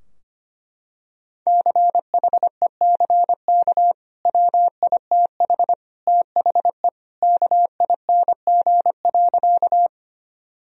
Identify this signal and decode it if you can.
Morse code, heard as a keyed tone, about 25 wpm, 700 Hz. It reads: CHECK WITH THE KING.
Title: Sentences from Top 1000 Words - Review Part 3 - 25wpm